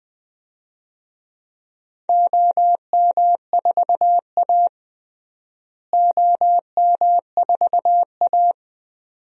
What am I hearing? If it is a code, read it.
Morse code, heard as a keyed tone, about 20 wpm, 700 Hz. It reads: OM4A OM4A
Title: DXCC-100 Call Signs - 4 Characters - Repeated 2x - 20wpm